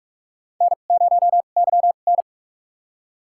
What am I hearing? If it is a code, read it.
Morse code, heard as a keyed tone, about 46 wpm, spaced as for 32 wpm, 700 Hz. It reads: N0YN
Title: USA Call Signs - 2x Character Spacing - 45wpm